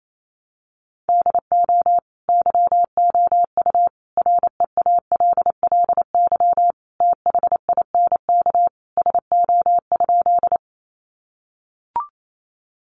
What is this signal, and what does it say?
Morse code, heard as a keyed tone, about 28 wpm, 700 Hz. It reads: DO YOU REALLY THINK SO?
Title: Sentences from Top 600 Words 28wpm